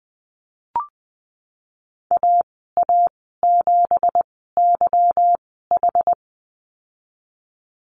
Morse code, heard as a keyed tone, about 20 wpm, spaced as for 15 wpm, 700 Hz. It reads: AA7YH
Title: 231 Lesson 37g - Letter x - Call Signs 20wpm